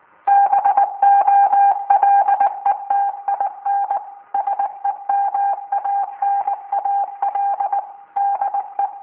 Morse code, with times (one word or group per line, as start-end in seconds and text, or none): 0.3-4.0 BOLETIN
4.3-7.8 SEMANAL
8.2-8.9 DE